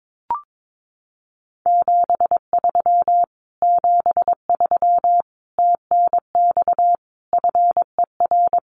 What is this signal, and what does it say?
Morse code, heard as a keyed tone, about 22 wpm, 700 Hz. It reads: 73 73 TNX FER
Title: Common QSO Phrases Encore 22wpm